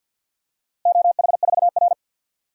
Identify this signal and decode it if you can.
Morse code, heard as a keyed tone, about 48 wpm, 700 Hz. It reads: OH4R
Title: DXCC-100 Call Signs - 4 Characters - Repeated 3x - 50wpm